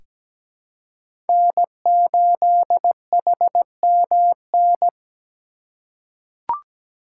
Morse code, heard as a keyed tone, about 17 wpm, 700 Hz. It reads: N8HMN
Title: USA Call Signs - 5 Characters - 3x Character Spacing - 17wpm